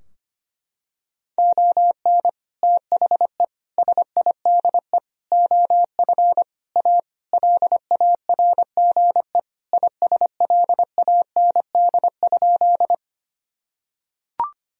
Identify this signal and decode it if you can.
Morse code, heard as a keyed tone, about 25 wpm, 700 Hz. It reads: ON THE SIDE OF A LARGE ISLAND?
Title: Sentences from Top 1000 Words - Review Part 3 - 25wpm